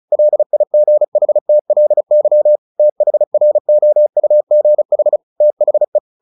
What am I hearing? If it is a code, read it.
Morse code, heard as a keyed tone, about 34 wpm, 600 Hz. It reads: LIGHTLY THROUGH THE
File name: the old man and the sea / the old man and the sea 1 35wpm